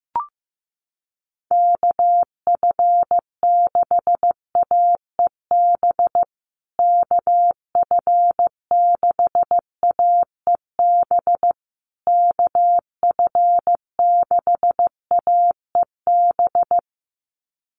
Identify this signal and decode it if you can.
Morse code, heard as a keyed tone, about 15 wpm, 700 Hz. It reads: KF6AEB KF6AEB KF6AEB
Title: USA Call Signs Repeated 3x 15wpm